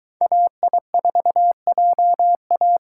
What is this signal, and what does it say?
Morse code, heard as a keyed tone, about 23 wpm, 700 Hz. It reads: AI4JA